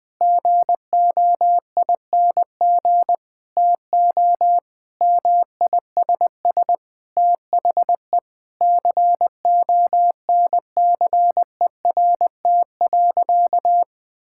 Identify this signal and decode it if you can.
Morse code, heard as a keyed tone, about 20 wpm, 700 Hz. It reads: GOING TO MISS THE CONCERT.
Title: Sentences with 8-Word Phrases - Spoken Beginning - 20wpm